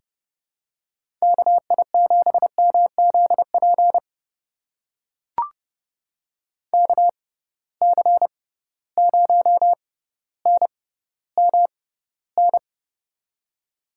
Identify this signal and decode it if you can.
Morse code, heard as a keyed tone, about 30 wpm, spaced as for 15 wpm, 700 Hz. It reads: KI7MZP KC0NMN
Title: USA Call Signs - 6 Characters - 6x Character Spacing - 30wpm